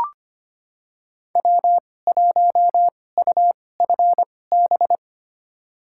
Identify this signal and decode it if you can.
Morse code, heard as a keyed tone, about 25 wpm, spaced as for 18 wpm, 700 Hz. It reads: W1UFB
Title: USA Call Signs - 5 Characters - 2x Character Spacing - 25wpm